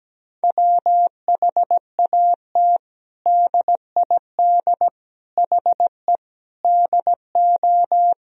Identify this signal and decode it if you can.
Morse code, heard as a keyed tone, about 17 wpm, 700 Hz. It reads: WHAT DID HE DO
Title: Sentences from Top 100 Words and Abbreviations 17wpm